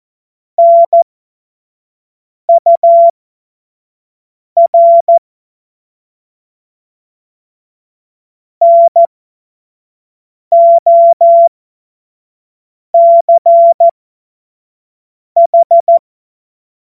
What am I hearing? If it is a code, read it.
Morse code, heard as a keyed tone, about 14 wpm, spaced as for 5 wpm, 680 Hz. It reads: NUR NOCH
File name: Witz-19-August-2025-25-F70